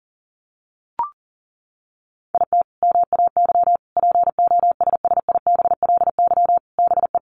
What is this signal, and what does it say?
Morse code, heard as a keyed tone, about 40 wpm, 700 Hz. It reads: IT MAY POSSIBLY BE
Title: Sentences from Top 1300 Words 40wpm